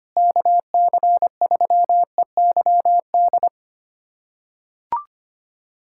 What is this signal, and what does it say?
Morse code, heard as a keyed tone, about 25 wpm, 700 Hz. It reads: KC3EYD
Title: USA Callsign DE Callsign 25wpm